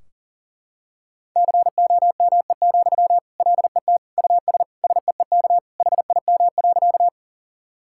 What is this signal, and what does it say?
Morse code, heard as a keyed tone, about 37 wpm, 700 Hz. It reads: COME, LET US SEEK HIM.
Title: Sentences from Top 1600 Words 40wpm